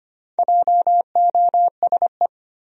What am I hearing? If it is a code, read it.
Morse code, heard as a keyed tone, about 25 wpm, 700 Hz. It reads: JOSE
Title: USA Common Names - Max 5 Letters - 25wpm